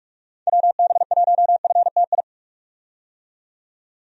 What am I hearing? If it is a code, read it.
Morse code, heard as a keyed tone, about 41 wpm, 700 Hz. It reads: WB1FTI
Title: USA Call Signs - 6 Characters - 45wpm